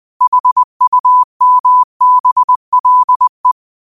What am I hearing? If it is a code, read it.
Morse code, heard as a keyed tone, about 20 wpm, 1000 Hz. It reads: HUMBLE